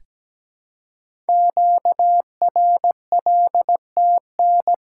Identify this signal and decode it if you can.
Morse code, heard as a keyed tone, about 17 wpm, 700 Hz. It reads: QRLTN
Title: Code Groups - 5 Letters - 17wpm